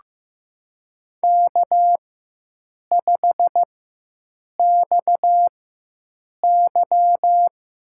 Morse code, heard as a keyed tone, about 15 wpm, spaced as for 7 wpm, 700 Hz. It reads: K5XY
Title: USA Call Signs - 4 Characters - 4x Character Spacing - 15wpm